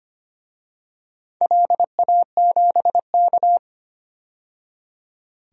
Morse code, heard as a keyed tone, about 25 wpm, 700 Hz. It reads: LA7K
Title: DXCC-100 Call Signs - 4 Characters - 25wpm